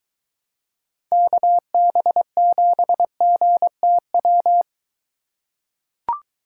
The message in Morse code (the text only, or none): KB7GTW